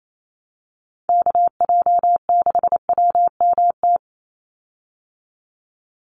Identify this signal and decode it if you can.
Morse code, heard as a keyed tone, about 28 wpm, 700 Hz. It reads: KJ6WMT
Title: USA Call Signs 28wpm